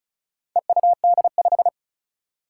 Morse code, heard as a keyed tone, about 36 wpm, 700 Hz. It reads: EUD5
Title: Four Challenging Letter-Number Combinations 35wpm